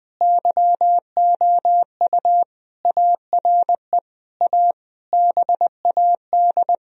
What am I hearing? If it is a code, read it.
Morse code, heard as a keyed tone, about 21 wpm, 700 Hz. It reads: YOU ARE A BAD
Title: Sentences from Top 1000 Words - Review Part 3 - 20wpm